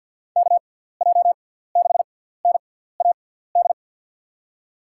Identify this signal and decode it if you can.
Morse code, heard as a keyed tone, about 52 wpm, spaced as for 17 wpm, 700 Hz. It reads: KJ6NAD
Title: USA Call Signs - 6 Characters - 6x Character Spacing - 50wpm